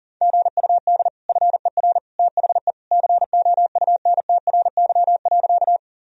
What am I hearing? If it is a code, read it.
Morse code, heard as a keyed tone, about 40 wpm, 700 Hz. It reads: GUD FER THE COUNTRY.
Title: Sentences from Top 300 Words and Abbreviations 40wpm